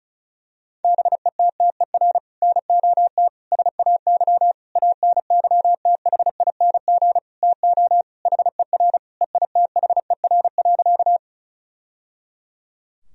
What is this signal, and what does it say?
Morse code, heard as a keyed tone, about 34 wpm, 700 Hz. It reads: BETTER NOT SAY ANYTHING TO HER EITHER.
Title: Sentences from Top 1000 Words - Review Part 4 - 35wpm